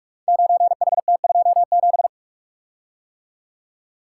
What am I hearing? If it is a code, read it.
Morse code, heard as a keyed tone, about 44 wpm, 700 Hz. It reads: 9HT27
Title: Code Groups - 5 Letter:Numbers - 45wpm